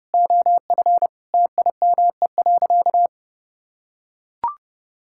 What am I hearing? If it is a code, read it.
Morse code, heard as a keyed tone, about 30 wpm, 700 Hz. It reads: OF TIME.
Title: Sentences from Top 1000 Words - Review Part 5 - 30wpm